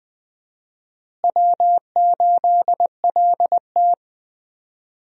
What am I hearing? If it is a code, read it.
Morse code, heard as a keyed tone, about 20 wpm, 700 Hz. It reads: W8LT
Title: USA Call Signs - 4 Characters - Repeated 3x - 20wpm